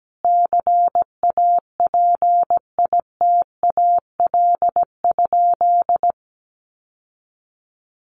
Morse code, heard as a keyed tone, about 17 wpm, 700 Hz. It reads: CAPITAL?
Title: Sentences from Top 1100 Words 17wpm